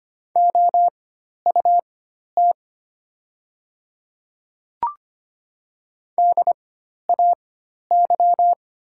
Morse code, heard as a keyed tone, about 25 wpm, spaced as for 12 wpm, 700 Hz. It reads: OUT DAY
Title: 4x Character Spacing - Top 100 Words - 25wpm